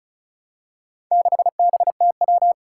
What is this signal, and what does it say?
Morse code, heard as a keyed tone, about 36 wpm, 700 Hz. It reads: 6BTW
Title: Four Challenging Letter-Number Combinations 35wpm